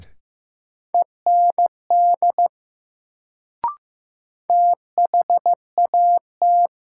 Morse code, heard as a keyed tone, about 15 wpm, 700 Hz. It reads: END THAT
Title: Sets of 2 Words - New Word Drill - 15wpm